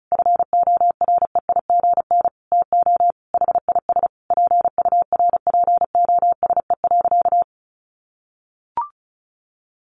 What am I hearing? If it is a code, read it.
Morse code, heard as a keyed tone, about 34 wpm, 700 Hz. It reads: FOREIGN TO HIS PURPOSE.
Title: Sentences from Top 1400 Words 35wpm